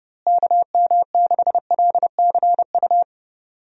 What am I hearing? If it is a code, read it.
Morse code, heard as a keyed tone, about 30 wpm, 700 Hz. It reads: KM6LCU
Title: USA Call Signs - 6 Characters - Repeated 3x - 30wpm